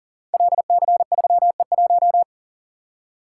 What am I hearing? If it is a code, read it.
Morse code, heard as a keyed tone, about 39 wpm, 700 Hz. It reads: LC3E1
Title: Code Groups - 5 Letter:Numbers - 40wpm